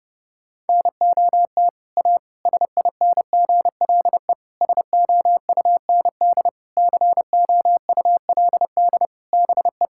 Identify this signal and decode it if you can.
Morse code, heard as a keyed tone, about 32 wpm, 700 Hz. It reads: NOT A SINGLE SOUND COULD BE
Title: Sentences from Top 1000 Words - Review Part 5 - 30wpm